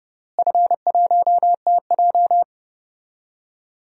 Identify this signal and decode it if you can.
Morse code, heard as a keyed tone, about 28 wpm, 700 Hz. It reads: F1TJ
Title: DXCC-100 Call Signs - 4 Characters - Repeated 3x - 30wpm